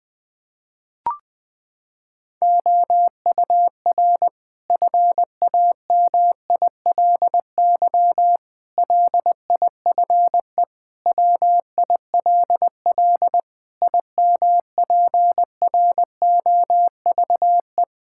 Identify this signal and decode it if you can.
Morse code, heard as a keyed tone, about 21 wpm, 700 Hz. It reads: OUR FAMILY LIFE WILL IMPROVE
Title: Sentences from Top 1000 Words - Review Part 5 - 20wpm